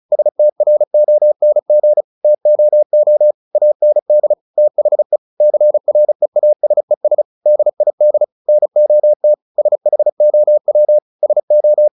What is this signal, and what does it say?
Morse code, heard as a keyed tone, about 36 wpm, 600 Hz. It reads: STRONG TOO AND THE CREASES DID NOT SHOW SO
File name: the old man and the sea / the old man and the sea 1 35wpm